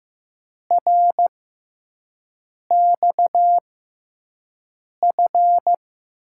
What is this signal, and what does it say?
Morse code, heard as a keyed tone, about 15 wpm, spaced as for 5 wpm, 700 Hz. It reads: RXF
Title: USA Call Signs - 6 Characters - 6x Character Spacing - 15wpm